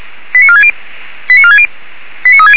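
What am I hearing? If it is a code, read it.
Five-tone selective call (ZVEI), heard as quick runs of five tones, tones of about 70 ms. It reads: 89460 89460 89360
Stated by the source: ZVEI five-tone call 89332 received by SDR.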